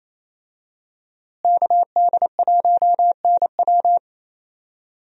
Morse code, heard as a keyed tone, about 28 wpm, 700 Hz. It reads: KD1NW
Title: USA Call Signs - 5 Characters - Repeated 3x - 28wpm